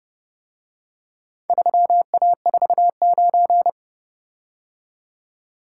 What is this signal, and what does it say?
Morse code, heard as a keyed tone, about 29 wpm, 700 Hz. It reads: 3A49